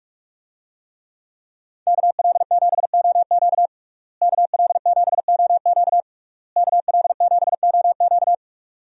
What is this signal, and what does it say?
Morse code, heard as a keyed tone, about 46 wpm, 700 Hz. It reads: KL7OQ KL7OQ KL7OQ
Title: DXCC-100 Call Signs - 5 Characters - Repeated 3x - 45wpm